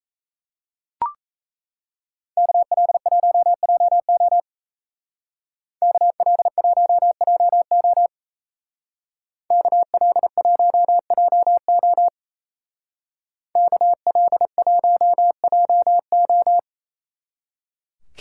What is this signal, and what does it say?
Morse code, heard as a keyed tone, about 35 wpm, 700 Hz. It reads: KL1JO KL1JO KL1JO KL1JO
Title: Speed Racing - USA 2x2 Call Signs 28wpm